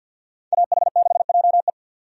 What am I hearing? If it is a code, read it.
Morse code, heard as a keyed tone, about 52 wpm, 700 Hz. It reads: AH6JE
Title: USA Call Signs - 5 Characters - 50wpm